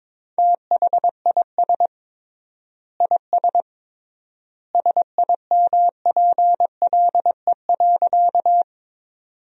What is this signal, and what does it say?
Morse code, heard as a keyed tone, about 22 wpm, 700 Hz. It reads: THIS IS SIMPLE.
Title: 3x Word-Spacing - Sentences from Top 400 Words - 22wpm